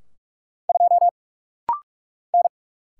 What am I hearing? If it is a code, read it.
Morse code, heard as a keyed tone, about 46 wpm, 700 Hz. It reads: 2 N
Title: Single Letter-Number - Rapid-Fire 45wpm